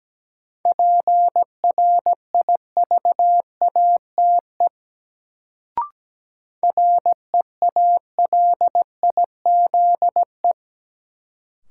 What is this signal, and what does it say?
Morse code, heard as a keyed tone, about 17 wpm, 700 Hz. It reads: PRIVATE REALIZE
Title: Sentences from Top 1000 Words - New Word Drill - 17wpm